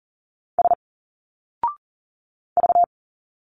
Morse code, heard as a keyed tone, about 40 wpm, 700 Hz. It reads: S V
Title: Single Letters 40wpm